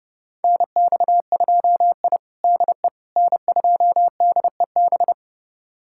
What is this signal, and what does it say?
Morse code, heard as a keyed tone, about 30 wpm, 700 Hz. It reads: NX2I DE N2DEB
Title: USA Callsign DE Callsign 30wpm